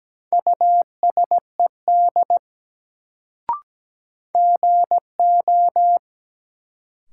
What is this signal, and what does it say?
Morse code, heard as a keyed tone, about 17 wpm, 700 Hz. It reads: USED GO
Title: Sets of 2 Words - New Word Drill - 17wpm